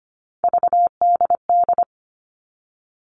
Morse code, heard as a keyed tone, about 25 wpm, 700 Hz. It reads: VDD
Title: All Cut Numbers - 3-Digit 25wpm